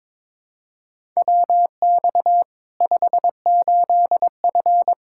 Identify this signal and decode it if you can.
Morse code, heard as a keyed tone, about 22 wpm, 700 Hz. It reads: WX 58F
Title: Practice QSO - Exchange 2 - Info - 22wpm